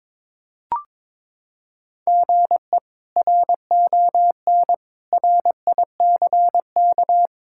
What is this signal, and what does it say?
Morse code, heard as a keyed tone, about 22 wpm, 700 Hz. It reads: GE RON RICK
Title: K1USN SST QSO Practice - 22wpm